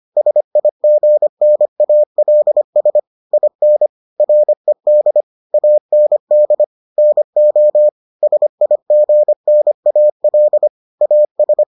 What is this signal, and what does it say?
Morse code, heard as a keyed tone, about 25 wpm, 600 Hz. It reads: SIGNALS IN RED AND NO SIGNAL AS